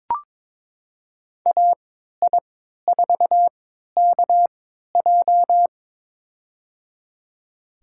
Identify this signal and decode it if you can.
Morse code, heard as a keyed tone, about 22 wpm, spaced as for 13 wpm, 700 Hz. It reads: AI4KJ
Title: USA Call Signs - 5 Characters - 3x Character Spacing - 22wpm